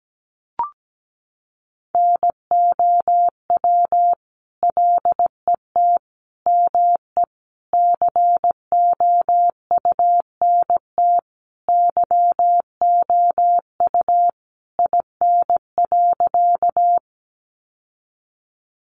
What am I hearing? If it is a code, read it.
Morse code, heard as a keyed tone, about 17 wpm, 700 Hz. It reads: NOW LET ME COUNT YOU IN.
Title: Sentences from Top 1500 Words 17wpm